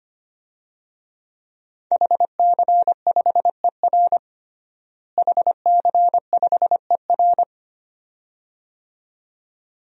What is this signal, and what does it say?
Morse code, heard as a keyed tone, about 25 wpm, 700 Hz. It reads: HC5ER HC5ER
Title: DXCC-100 Call Signs - 5 Characters - Repeated 2x - 25wpm